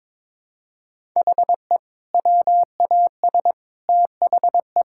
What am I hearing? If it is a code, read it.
Morse code, heard as a keyed tone, about 22 wpm, 700 Hz. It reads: HE WAS THE
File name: practice / 22wpm / Sentences with 2-Word Phrases - 22wpm